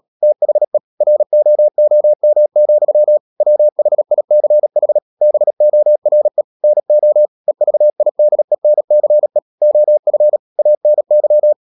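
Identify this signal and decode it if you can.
Morse code, heard as a keyed tone, about 37 wpm, 600 Hz. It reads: THE ROOM, WHICH BORE NO EVIDENCE OF ANY
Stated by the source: Morse code practice at 35–39 wpm sending English text of Ken Follett's novel Never, from CW med Gnister.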